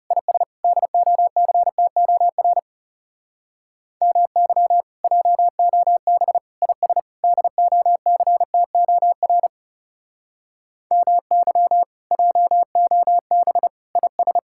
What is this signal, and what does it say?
Morse code, heard as a keyed tone, about 34 wpm, 700 Hz. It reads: IS DOCTOR MY JOB IS DOCTOR MY JOB IS
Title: Speed-Racing Common QSO Phrases - 30wpm